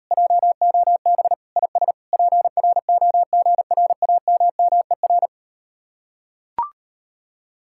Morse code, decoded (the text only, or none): JOB IS PROGRAMMER